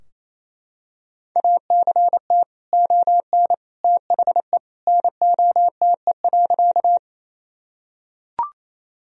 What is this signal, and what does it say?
Morse code, heard as a keyed tone, about 28 wpm, 700 Hz. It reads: ACT ON THE NOTE.
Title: Sentences from Top 1000 Words - Review Part 2 - 28wpm